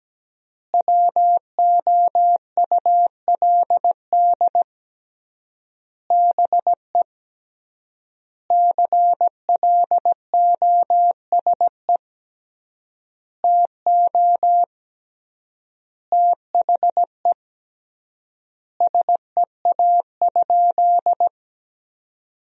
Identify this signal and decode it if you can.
Morse code, heard as a keyed tone, about 17 wpm, 700 Hz. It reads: WOULD BE CLOSE TO THE SEA?
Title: 3x Word-Spacing - Sentences from Top 300 Words - 17wpm